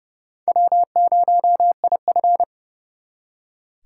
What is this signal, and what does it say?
Morse code, heard as a keyed tone, about 38 wpm, 700 Hz. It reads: W0IF
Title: Speed Racing - USA Call Signs 30wpm